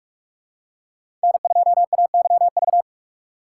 Morse code, heard as a keyed tone, about 46 wpm, 700 Hz. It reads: N2AYV
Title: USA Call Signs - 5 Characters - 2x Character Spacing - 45wpm